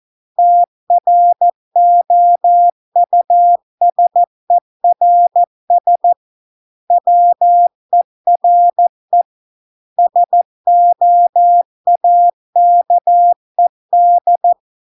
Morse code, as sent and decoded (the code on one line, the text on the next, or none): - .-. --- ..- ... .-. ... / .-- . .-. / ... --- .- -.- . -..
TROUSERS WERE SOAKED